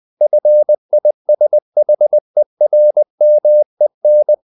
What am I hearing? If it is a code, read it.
Morse code, heard as a keyed tone, about 20 wpm, 600 Hz. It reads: FISHERMEN
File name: the old man and the sea part 2 20wpm